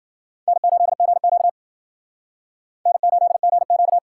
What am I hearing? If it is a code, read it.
Morse code, heard as a keyed tone, about 52 wpm, 700 Hz. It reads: N8GQ N8GQ